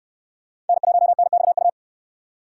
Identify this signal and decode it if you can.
Morse code, heard as a keyed tone, about 69 wpm, 700 Hz. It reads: N1NXU